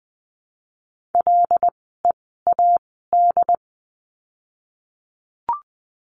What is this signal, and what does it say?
Morse code, heard as a keyed tone, about 20 wpm, spaced as for 14 wpm, 700 Hz. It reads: LEAD
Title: 075 Lesson 13f - Letter L - Words 20wpm